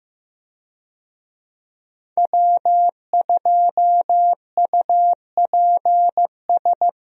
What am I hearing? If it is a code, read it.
Morse code, heard as a keyed tone, about 15 wpm, 700 Hz. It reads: W2UPS
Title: USA Call Signs - 5 Characters - 15wpm